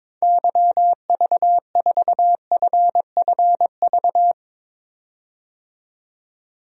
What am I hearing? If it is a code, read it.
Morse code, heard as a keyed tone, about 22 wpm, 700 Hz. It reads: YV4FFV
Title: DXCC-100 Call Signs - 6 Characters - 22wpm